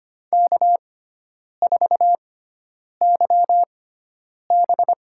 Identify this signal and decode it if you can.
Morse code, heard as a keyed tone, about 25 wpm, spaced as for 9 wpm, 700 Hz. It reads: K4YB
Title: USA Call Signs - 4 Characters - 6x Character Spacing - 25wpm